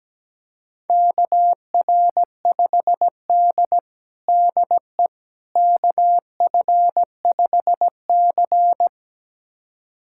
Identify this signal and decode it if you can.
Morse code, heard as a keyed tone, about 17 wpm, 700 Hz. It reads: KR5D DE KF5C
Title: USA Callsign DE Callsign - 4 Characters - 17wpm